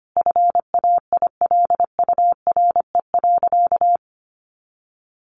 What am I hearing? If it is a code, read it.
Morse code, heard as a keyed tone, about 25 wpm, 700 Hz. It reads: FAILURE.
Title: Sentences from Top 1300 Words 25wpm